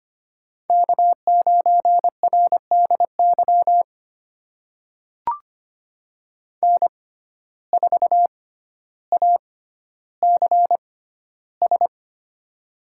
Morse code, heard as a keyed tone, about 25 wpm, spaced as for 13 wpm, 700 Hz. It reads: K9RDY N4ACS
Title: USA Call Signs - 5 Characters - 6x Character Spacing - 25wpm